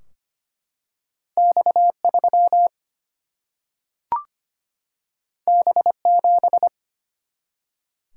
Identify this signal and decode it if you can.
Morse code, heard as a keyed tone, about 25 wpm, 700 Hz. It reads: X3 B7
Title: Single-Letter Single-Number 25wpm